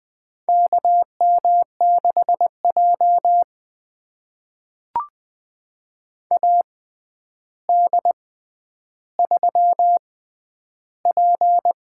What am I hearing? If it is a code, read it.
Morse code, heard as a keyed tone, about 20 wpm, spaced as for 10 wpm, 700 Hz. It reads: KM6J AD3P
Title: USA Call Signs - 4 Characters - 6x Character Spacing - 20wpm